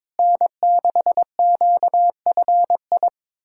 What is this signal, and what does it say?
Morse code, heard as a keyed tone, about 22 wpm, 700 Hz. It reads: N6QFI
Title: USA Callsign DE Callsign - 5 Characters - 22wpm